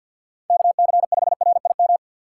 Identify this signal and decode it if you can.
Morse code, heard as a keyed tone, about 53 wpm, 700 Hz. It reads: KC5RIM
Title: USA Call Signs - 6 Characters - 3x Character Spacing - 50wpm